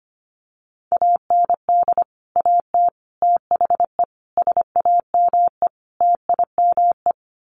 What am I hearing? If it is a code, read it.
Morse code, heard as a keyed tone, about 25 wpm, 700 Hz. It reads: AND AT THE SAME TIME
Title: Sets of 5 Words 25wpm